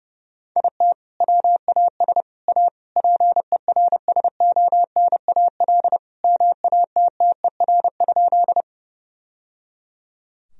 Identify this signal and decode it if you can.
Morse code, heard as a keyed tone, about 30 wpm, 700 Hz. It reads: IT WAS A PERSONAL MATTER?
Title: Sentences from Top 1000 Words - Review Part 4 - 30wpm